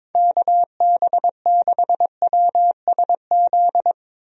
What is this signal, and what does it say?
Morse code, heard as a keyed tone, about 22 wpm, 700 Hz. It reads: KB6WSZ
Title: USA Call Signs - 6 Characters - 22wpm